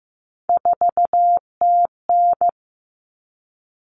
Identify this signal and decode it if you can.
Morse code, heard as a keyed tone, about 15 wpm, 700 Hz. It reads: 4TN
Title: All Cut Numbers - 3-Digit 15wpm